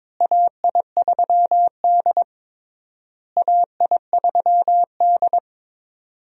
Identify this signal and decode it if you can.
Morse code, heard as a keyed tone, about 21 wpm, 700 Hz. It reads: AI3D AI3D
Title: USA Call Signs - 4 Characters - Repeated 3x - 22wpm